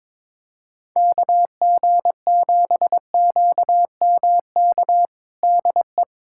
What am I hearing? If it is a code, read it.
Morse code, heard as a keyed tone, about 22 wpm, 700 Hz. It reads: KG7QMK DE
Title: USA Callsign DE Callsign - 6 Characters - 22wpm